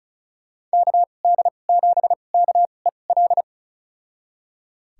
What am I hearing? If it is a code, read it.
Morse code, heard as a keyed tone, about 36 wpm, spaced as for 25 wpm, 700 Hz. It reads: KD7KEL
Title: USA Call Signs - 6 Characters - 2x Character Spacing - 35wpm